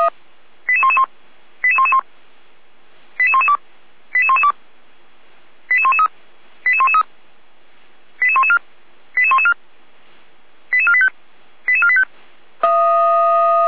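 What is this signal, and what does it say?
Five-tone selective call (ZVEI), heard as quick runs of five tones, tones of about 70 ms. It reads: 88111 88111 88112 88112 88113 88113 88114 88114 88485 88485
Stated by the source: ZVEI five-tone call received by SDR, from audio.f85360.feuerwehr-orlamuende.org.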